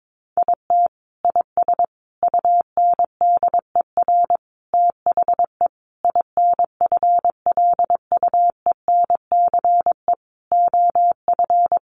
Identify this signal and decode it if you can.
Morse code, heard as a keyed tone, about 22 wpm, 700 Hz. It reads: IT IS UNDER THE INFLUENCE OF
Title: Sentences from Top 1400 Words 22wpm